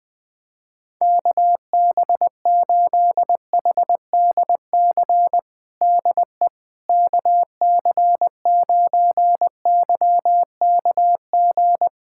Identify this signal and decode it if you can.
Morse code, heard as a keyed tone, about 21 wpm, 700 Hz. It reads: KB8HDC DE KC9YKG
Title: USA Callsign DE Callsign - 6 Characters - 20wpm